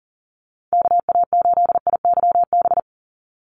Morse code, heard as a keyed tone, about 40 wpm, 700 Hz. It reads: KA8IYB
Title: USA Call Signs 40wpm